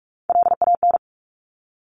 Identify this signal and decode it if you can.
Morse code, heard as a keyed tone, about 46 wpm, 700 Hz. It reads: LAN
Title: Top Three Letter Combinations in Words 45wpm